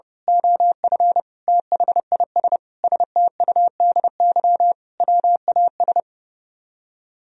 Morse code, heard as a keyed tone, about 28 wpm, 700 Hz. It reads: OF THIS STUDY WAS
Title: Speed-Racing - Sets of 5 Words 30wpm